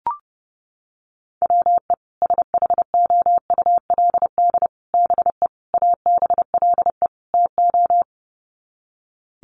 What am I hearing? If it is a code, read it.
Morse code, heard as a keyed tone, about 30 wpm, 700 Hz. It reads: WE SHOULD BE ABLE TO
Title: Sets of 5 Words Encore2 30wpm